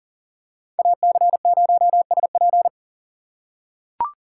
Morse code, heard as a keyed tone, about 40 wpm, 700 Hz. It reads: AC0SP
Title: USA Call Signs - 5 Characters - 4x Character Spacing - 40wpm